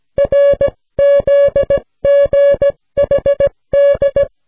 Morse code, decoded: RZGHD